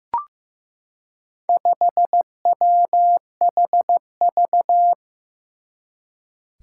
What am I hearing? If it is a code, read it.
Morse code, heard as a keyed tone, about 15 wpm, 700 Hz. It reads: 5WHV
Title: Four Challenging Letter-Number Combinations 15wpm